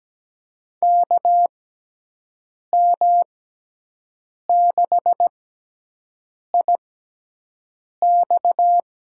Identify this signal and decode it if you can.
Morse code, heard as a keyed tone, about 17 wpm, spaced as for 6 wpm, 700 Hz. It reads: KM6IX
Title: USA Call Signs - 6x Character Spacing - 17wpm